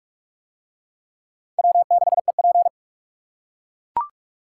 Morse code, heard as a keyed tone, about 41 wpm, 700 Hz. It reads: W6EP